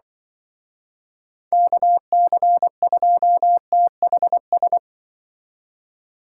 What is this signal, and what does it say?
Morse code, heard as a keyed tone, about 23 wpm, 700 Hz. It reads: KC2THS
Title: Speed Racing - USA Call Signs 20wpm